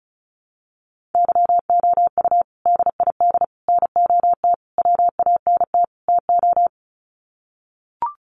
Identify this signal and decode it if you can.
Morse code, heard as a keyed tone, about 34 wpm, 700 Hz. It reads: YOU DID NOT WANT TO
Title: Sets of 5 Words Encore2 35wpm